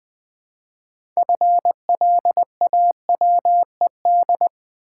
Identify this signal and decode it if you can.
Morse code, heard as a keyed tone, about 20 wpm, 700 Hz. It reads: FLAWED
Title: 6 Letter Words - First 3 Letters Given - 20wpm